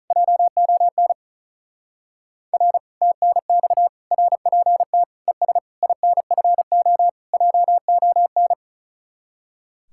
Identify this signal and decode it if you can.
Morse code, heard as a keyed tone, about 43 wpm, 700 Hz. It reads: JON R TNX RPT ES INFO JON
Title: Speed-Racing Common QSO Phrases - 35wpm